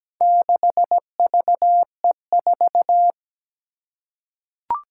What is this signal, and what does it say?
Morse code, heard as a keyed tone, about 17 wpm, 700 Hz. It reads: 6VE4
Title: Four Challenging Letter-Number Combinations 17wpm